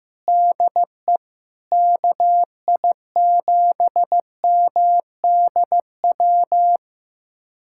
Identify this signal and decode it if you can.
Morse code, heard as a keyed tone, about 15 wpm, 700 Hz. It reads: DE KI7MDW
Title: USA Callsign DE Callsign - 6 Characters - 15wpm